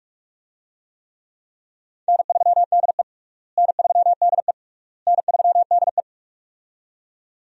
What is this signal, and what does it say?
Morse code, heard as a keyed tone, about 46 wpm, 700 Hz. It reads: N3DE N3DE N3DE
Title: USA Call Signs - 4 Characters - Repeated 3x - 45wpm